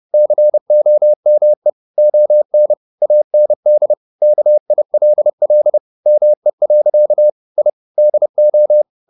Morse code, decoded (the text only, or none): COME ON AND KILL ME. I DO